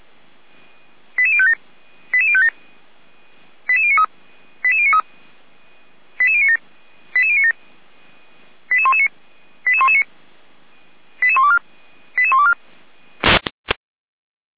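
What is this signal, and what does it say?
Five-tone selective call (ZVEI), heard as quick runs of five tones, tones of about 70 ms. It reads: 88057 88057 88093 88093 88097 88097 88119 88119 88124 88124
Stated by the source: ZVEI five-tone call 88053 received on an SDR, logged by the Orlamünde fire brigade (feuerwehr-orlamuende).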